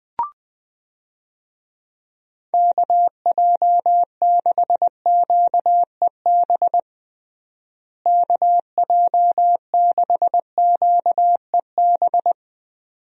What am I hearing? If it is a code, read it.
Morse code, heard as a keyed tone, about 20 wpm, 700 Hz. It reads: KJ6QEB KJ6QEB
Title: USA Call Signs - 6 Characters - Repeated 2x - 20wpm